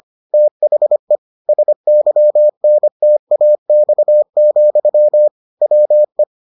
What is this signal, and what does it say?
Morse code, heard as a keyed tone, about 25 wpm, 600 Hz. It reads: THE SYNTAX, WE